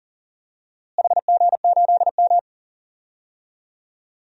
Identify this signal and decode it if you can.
Morse code, heard as a keyed tone, about 40 wpm, 700 Hz. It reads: HG8M
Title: DXCC-100 Call Signs - 4 Characters - Repeated 3x - 40wpm